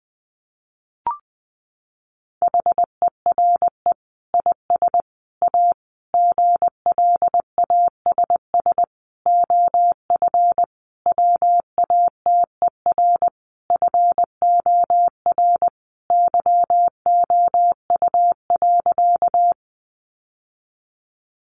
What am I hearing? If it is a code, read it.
Morse code, heard as a keyed tone, about 19 wpm, 700 Hz. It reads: HERE IS A GLASS OF WATER FOR YOU.
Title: Sentences from Top 1300 Words 20wpm